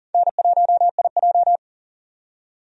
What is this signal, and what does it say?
Morse code, heard as a keyed tone, about 40 wpm, 700 Hz. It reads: N1IJ
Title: USA Call Signs - 4 Characters - 40wpm